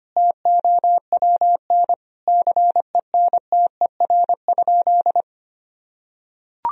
Morse code, heard as a keyed tone, about 25 wpm, 700 Hz. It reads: TOWN CENTER?
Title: Sentences from Top 1000 Words - Review Part 2 - 25wpm